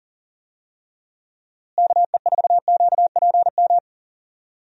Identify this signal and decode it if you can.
Morse code, heard as a keyed tone, about 39 wpm, 700 Hz. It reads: KE4QPM
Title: USA Call Signs - 6 Characters - Repeated 3x - 40wpm